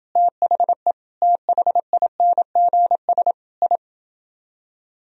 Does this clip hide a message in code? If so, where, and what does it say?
Yes: Morse code from 0.2 to 3.8 s, heard as a keyed tone, about 29 wpm, 700 Hz. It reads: THE THINGS I